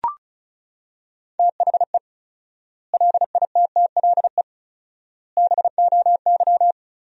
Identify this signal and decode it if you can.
Morse code, heard as a keyed tone, about 36 wpm, 700 Hz. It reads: THE LITTLE BOY
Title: 4x Word-Spacing - Sentences from Top 500 Words - 35wpm